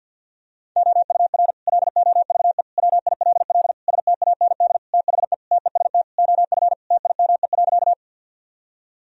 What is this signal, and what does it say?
Morse code, heard as a keyed tone, about 46 wpm, 700 Hz. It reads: OUR LOVE WILL STAND THE TEST OF TIME.
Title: Sentences from Top 1000 Words - Review Part 3 - 50wpm